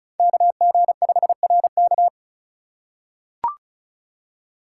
Morse code, heard as a keyed tone, about 36 wpm, 700 Hz. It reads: KG5RK
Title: USA Call Signs - 5 Characters - 3x Character Spacing - 35wpm